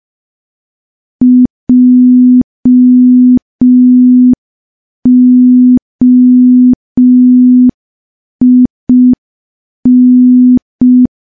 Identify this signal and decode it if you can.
Morse code, heard as a keyed tone, about 5 wpm, 250 Hz. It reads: JOIN
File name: morse / morse-training-05WPM